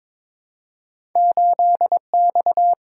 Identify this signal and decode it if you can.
Morse code, heard as a keyed tone, about 22 wpm, 700 Hz. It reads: 8X
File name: practice / 22wpm / Two Letter-Number Combinations - 22wpm